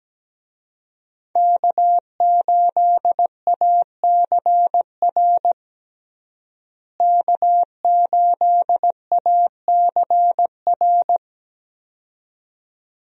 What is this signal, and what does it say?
Morse code, heard as a keyed tone, about 17 wpm, 700 Hz. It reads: K8ACR K8ACR
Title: USA Call Signs - 5 Characters - Repeated 2x - 17wpm